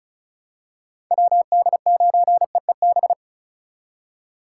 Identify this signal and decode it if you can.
Morse code, heard as a keyed tone, about 36 wpm, 700 Hz. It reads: WD9EEB
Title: USA Call Signs - 6 Characters - 4x Character Spacing - 35wpm